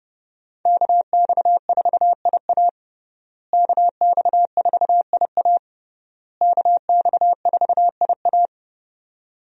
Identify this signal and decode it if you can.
Morse code, heard as a keyed tone, about 30 wpm, 700 Hz. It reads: KX4IA KX4IA KX4IA
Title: USA Call Signs - 5 Characters - Repeated 3x - 30wpm